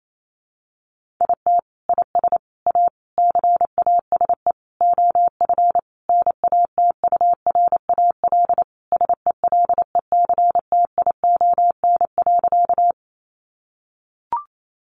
Morse code, heard as a keyed tone, about 28 wpm, 700 Hz. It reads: IT IS A CASE OF NATURAL SELECTION.
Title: Sentences from Top 1400 Words 28wpm